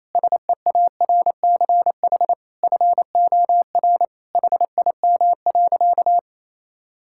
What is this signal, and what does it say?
Morse code, heard as a keyed tone, about 28 wpm, 700 Hz. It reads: SEARCH FOR HIM.
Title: Sentences from Top 1000 Words - Review Part 5 - 28wpm